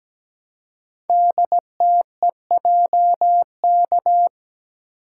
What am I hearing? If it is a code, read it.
Morse code, heard as a keyed tone, about 17 wpm, 700 Hz. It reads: DTEJK
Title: Code Groups - 5 Letters - 17wpm